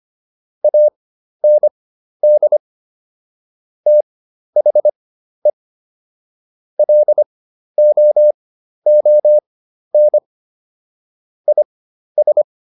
Morse code, heard as a keyed tone, about 25 wpm, spaced as for 12 wpm, 600 Hz. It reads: AND THE LOON IS